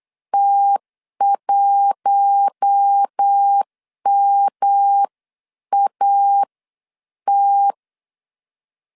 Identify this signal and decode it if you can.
Morse code, heard as a keyed tone, about 8 wpm, spaced as for 7 wpm, 800 Hz. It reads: T1MAT
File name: 8wpm